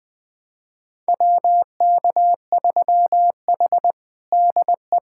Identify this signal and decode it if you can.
Morse code, heard as a keyed tone, about 20 wpm, 700 Hz. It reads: WK3H DE
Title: USA Callsign DE Callsign - 4 Characters - 20wpm